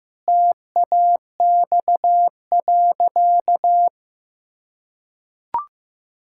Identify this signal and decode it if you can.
Morse code, heard as a keyed tone, about 16 wpm, 700 Hz. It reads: TAX.
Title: Sentences from Top 1000 Words - Review Part 6 - 15wpm